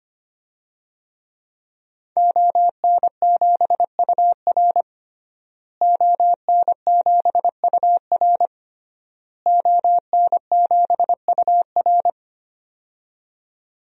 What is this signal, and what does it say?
Morse code, heard as a keyed tone, about 25 wpm, 700 Hz. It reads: ON7UR ON7UR ON7UR
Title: DXCC-100 Call Signs - 5 Characters - Repeated 3x - 25wpm